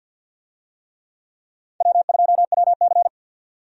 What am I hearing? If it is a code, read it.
Morse code, heard as a keyed tone, about 48 wpm, 700 Hz. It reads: W2WC
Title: USA Call Signs - 4 Characters - 50wpm